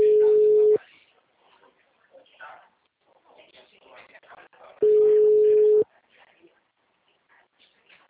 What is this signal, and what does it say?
Ringback tone (Europe, 425 Hz), with speech.